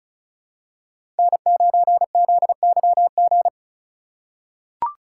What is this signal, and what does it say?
Morse code, heard as a keyed tone, about 34 wpm, 700 Hz. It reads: N9ZYG
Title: USA Call Signs - 6x Character Spacing - 35wpm